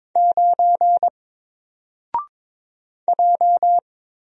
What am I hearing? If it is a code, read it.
Morse code, heard as a keyed tone, about 22 wpm, 700 Hz. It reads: 9 J
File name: Single-Letter-Number-22wpm